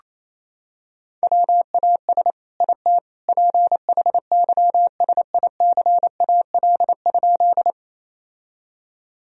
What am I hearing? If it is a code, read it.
Morse code, heard as a keyed tone, about 28 wpm, 700 Hz. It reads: WAS IT PHYSICAL?
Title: Sentences from Top 1000 Words - Review Part 4 - 28wpm